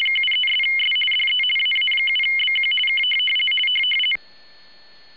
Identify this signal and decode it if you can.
RTTY, about 50 baud. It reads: 73'S DE AA7BQ K
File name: rtty50